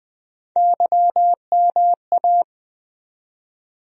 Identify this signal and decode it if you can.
Morse code, heard as a keyed tone, about 20 wpm, 700 Hz. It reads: YMA
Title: USA Call Sign Postfixes - 3 Letters 20wpm